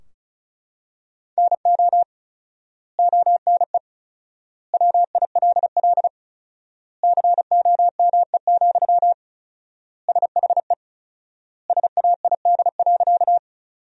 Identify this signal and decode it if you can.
Morse code, heard as a keyed tone, about 36 wpm, 700 Hz. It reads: NO ONE WILL COME, SHE SAID.
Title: 4x Word-Spacing - Sentences from Top 100 Words - 35wpm